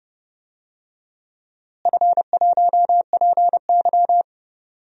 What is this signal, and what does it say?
Morse code, heard as a keyed tone, about 30 wpm, 700 Hz. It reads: F1PY
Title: DXCC-100 Call Signs - 4 Characters - Repeated 3x - 30wpm